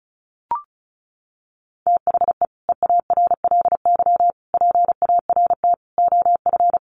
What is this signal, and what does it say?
Morse code, heard as a keyed tone, about 34 wpm, 700 Hz. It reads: THE EARLY PART OF
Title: Sets of 5 Words Encore2 35wpm